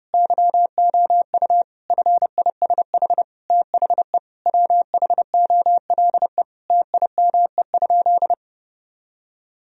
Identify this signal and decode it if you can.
Morse code, heard as a keyed tone, about 30 wpm, 700 Hz. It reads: YOU FISH THE WHOLE TIME?
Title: Sentences from Top 1000 Words - Review Part 1 - 30wpm